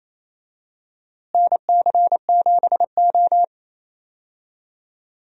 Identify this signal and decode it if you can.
Morse code, heard as a keyed tone, about 28 wpm, 700 Hz. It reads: NC7O
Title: USA Call Signs - 4 Characters - Repeated 3x - 28wpm